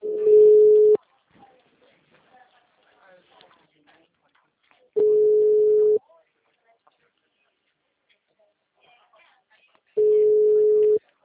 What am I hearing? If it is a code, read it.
Ringback tone (Europe, 425 Hz), with speech.